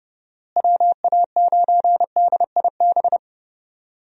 Morse code, heard as a keyed tone, about 30 wpm, 700 Hz. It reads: WA9DIB
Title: USA Call Signs - 6 Characters - Repeated 2x - 30wpm